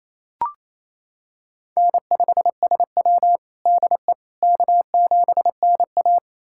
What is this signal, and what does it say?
Morse code, heard as a keyed tone, about 28 wpm, 700 Hz. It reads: N5SW DE K7NA
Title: USA Callsign DE Callsign - 4 Characters - 28wpm